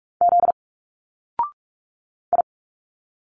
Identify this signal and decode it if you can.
Morse code, heard as a keyed tone, about 46 wpm, 700 Hz. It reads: Z I
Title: Single Letters 45wpm